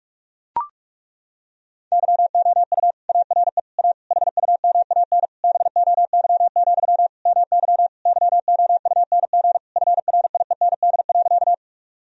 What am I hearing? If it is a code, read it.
Morse code, heard as a keyed tone, about 42 wpm, 700 Hz. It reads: YOU ARE A HUMAN BOY, MY YOUNG FRIEND.
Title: Sentences from Top 1000 Words - Review Part 3 - 45wpm